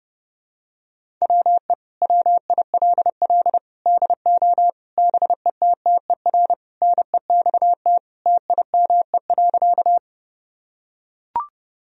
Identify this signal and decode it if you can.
Morse code, heard as a keyed tone, about 32 wpm, 700 Hz. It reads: WE WILL DO BETTER NEXT TIME.
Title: Sentences from Top 1000 Words - Review Part 1 - 30wpm